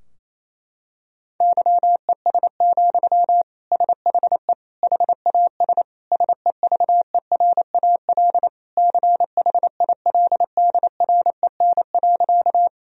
Morse code, heard as a keyed tone, about 28 wpm, 700 Hz. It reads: YES, SHE HAS SEVERAL CHILDREN.
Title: Sentences from Top 1000 Words - Review Part 2 - 28wpm